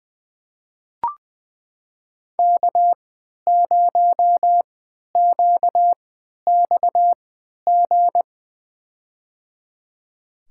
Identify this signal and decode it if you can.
Morse code, heard as a keyed tone, about 20 wpm, spaced as for 11 wpm, 700 Hz. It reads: K0QXG
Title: USA Call Signs - 3x Character Spacing - 20wpm